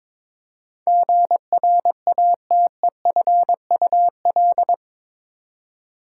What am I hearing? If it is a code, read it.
Morse code, heard as a keyed tone, about 22 wpm, 700 Hz. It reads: GRATEFUL
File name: practice / 22wpm / Eight Letter Words - 22wpm